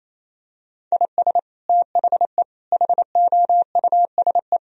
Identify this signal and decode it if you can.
Morse code, heard as a keyed tone, about 28 wpm, 700 Hz. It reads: IS THE HOUSE
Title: Sentences from Top 1000 Words - Review Part 1 - 28wpm